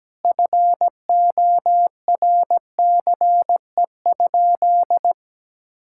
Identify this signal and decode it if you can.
Morse code, heard as a keyed tone, about 17 wpm, 700 Hz. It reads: FORCE?
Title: Sentences from Top 1600 Words 17wpm